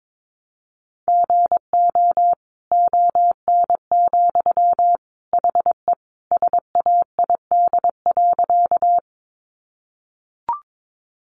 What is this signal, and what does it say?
Morse code, heard as a keyed tone, about 22 wpm, 700 Hz. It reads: GO ON, HE SAID.
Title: Sentences from Top 100 Words 22wpm